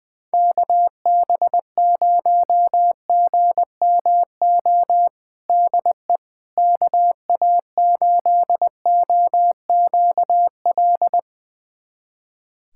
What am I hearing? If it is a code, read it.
Morse code, heard as a keyed tone, about 21 wpm, 700 Hz. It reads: KB0GMO DE KA8OQL
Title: USA Callsign DE Callsign 20wpm